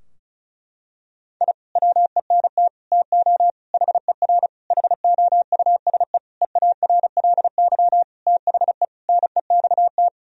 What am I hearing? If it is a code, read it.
Morse code, heard as a keyed tone, about 36 wpm, 700 Hz. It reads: I WENT TO HER HOUSE EARLY THE NEXT
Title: Sentences from Top 1000 Words - Review Part 2 - 35wpm